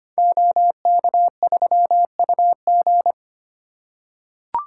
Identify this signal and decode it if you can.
Morse code, heard as a keyed tone, about 25 wpm, 700 Hz. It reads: OK3UG
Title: Code Groups - 5 Letter:Numbers - 25wpm